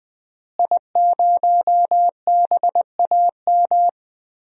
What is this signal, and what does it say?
Morse code, heard as a keyed tone, about 20 wpm, 700 Hz. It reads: I0BAM
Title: DXCC-100 Call Signs - 5 Characters - Repeated 3x - 20wpm